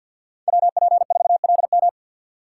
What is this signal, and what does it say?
Morse code, heard as a keyed tone, about 52 wpm, 700 Hz. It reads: WP4LM
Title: USA Call Signs - 5 Characters - 2x Character Spacing - 50wpm